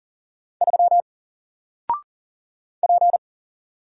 Morse code, heard as a keyed tone, about 40 wpm, 700 Hz. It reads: 3 P
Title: Single Letter-Number Alt 40wpm